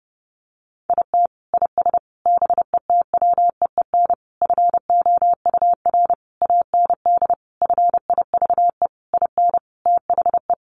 Morse code, heard as a keyed tone, about 30 wpm, 700 Hz. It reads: IT IS BETWEEN FOUR AND FIVE IN THE
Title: Sentences from Top 1500 Words 30wpm